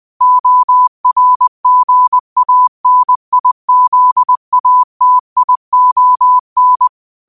Morse code, heard as a keyed tone, about 20 wpm, 1000 Hz. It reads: ORGANIZATION
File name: qotdfiles / 20wpm